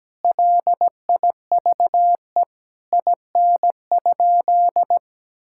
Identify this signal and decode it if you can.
Morse code, heard as a keyed tone, about 17 wpm, 700 Hz. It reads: LIVE IN?
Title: Sentences from Top 1000 Words - Review Part 5 - 17wpm